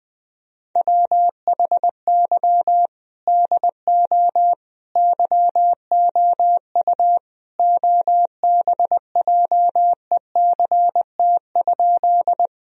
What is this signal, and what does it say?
Morse code, heard as a keyed tone, about 20 wpm, 700 Hz. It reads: WHY DO YOU OBJECT?
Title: Sentences from Top 1000 Words - Review Part 3 - 20wpm